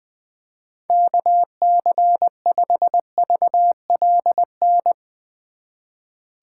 Morse code, heard as a keyed tone, about 20 wpm, 700 Hz. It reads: KC5VLN
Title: USA Call Signs - 6 Characters - 8x Character Spacing - 20wpm